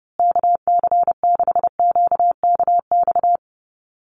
Morse code, heard as a keyed tone, about 29 wpm, 700 Hz. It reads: KC6QKX
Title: USA Call Signs Repeated 3x 30wpm